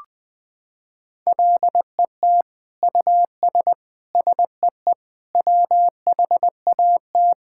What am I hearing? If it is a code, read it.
Morse code, heard as a keyed tone, about 20 wpm, 700 Hz. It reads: LET US SEE WHAT
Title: Sentences from Top 1000 Words - Review Part 5 - 20wpm